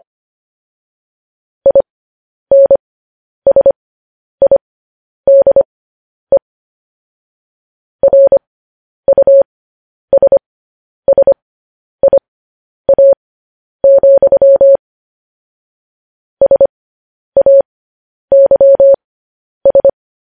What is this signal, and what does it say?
Morse code, heard as a keyed tone, about 25 wpm, spaced as for 10 wpm, 550 Hz. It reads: INSIDE RUSSIA, SAYS